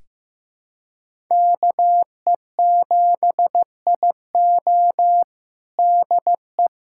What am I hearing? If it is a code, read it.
Morse code, heard as a keyed tone, about 16 wpm, 700 Hz. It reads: KE7IO DE
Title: USA Callsign DE Callsign - 5 Characters - 15wpm